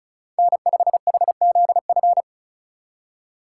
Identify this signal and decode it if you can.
Morse code, heard as a keyed tone, about 44 wpm, 700 Hz. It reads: N5HZF